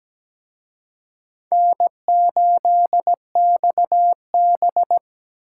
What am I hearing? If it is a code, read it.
Morse code, heard as a keyed tone, about 20 wpm, 700 Hz. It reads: N8XB
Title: Speed Racing - USA Call Signs 17wpm